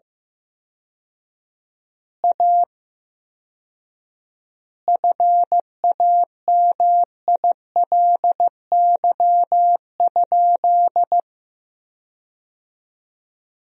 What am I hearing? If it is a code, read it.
Morse code, heard as a keyed tone, about 15 wpm, 700 Hz. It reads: A FAMILY?
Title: 4x Word-Spacing - Sentences from Top 400 Words - 15wpm